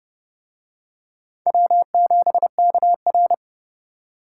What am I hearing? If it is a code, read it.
Morse code, heard as a keyed tone, about 28 wpm, 700 Hz. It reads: W7KR